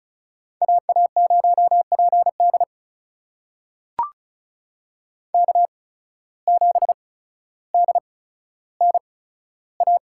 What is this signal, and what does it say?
Morse code, heard as a keyed tone, about 34 wpm, spaced as for 15 wpm, 700 Hz. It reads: AA0PD K7DNA
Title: USA Call Signs - 5 Characters - 8x Character Spacing - 35wpm